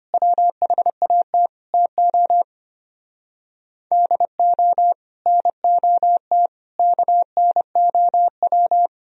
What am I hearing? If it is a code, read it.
Morse code, heard as a keyed tone, about 33 wpm, 700 Hz. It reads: WHAT TO DO NOT KNOW